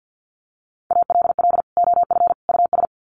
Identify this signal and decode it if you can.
Morse code, heard as a keyed tone, about 50 wpm, 700 Hz. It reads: ALL OF US
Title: Sets of 3 Words 50wpm